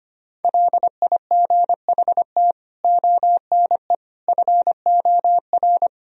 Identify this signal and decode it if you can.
Morse code, heard as a keyed tone, about 25 wpm, 700 Hz. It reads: LIGHT ONE FOR
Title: Sentences from Top 1000 Words - Review Part 1 - 25wpm